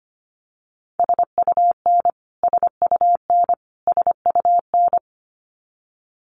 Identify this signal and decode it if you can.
Morse code, heard as a keyed tone, about 25 wpm, 700 Hz. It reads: SUN SUN SUN
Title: Top 300 Words Repeated 25wpm